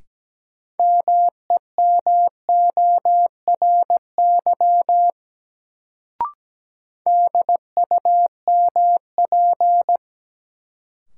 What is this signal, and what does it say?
Morse code, heard as a keyed tone, about 17 wpm, 700 Hz. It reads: MEMORY DUMP
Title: Sentences from Top 1300 Words - New Word Drill Alt - 17wpm